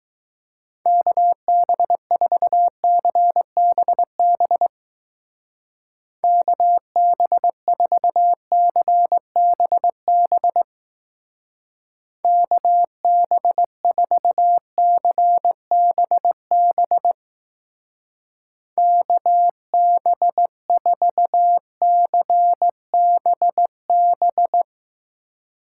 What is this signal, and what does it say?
Morse code, heard as a keyed tone, about 20 wpm, 700 Hz. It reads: KB4CBB KB4CBB KB4CBB KB4CBB